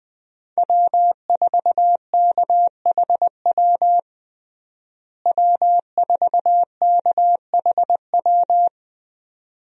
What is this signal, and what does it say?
Morse code, heard as a keyed tone, about 20 wpm, 700 Hz. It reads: W4KHW W4KHW
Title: USA Call Signs - 5 Characters - Repeated 2x - 20wpm